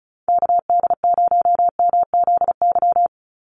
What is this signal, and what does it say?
Morse code, heard as a keyed tone, about 34 wpm, 700 Hz. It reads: KD0MZY
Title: USA Call Signs 35wpm